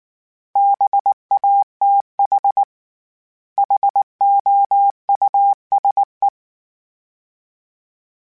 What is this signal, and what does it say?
Morse code, heard as a keyed tone, about 19 wpm, 800 Hz. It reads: BATH HOUSE
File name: General / General-mp3s / Houses @18